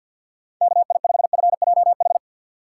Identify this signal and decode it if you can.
Morse code, heard as a keyed tone, about 49 wpm, 700 Hz. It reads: KI5FJH
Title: USA Call Signs - 6 Characters - 4x Character Spacing - 50wpm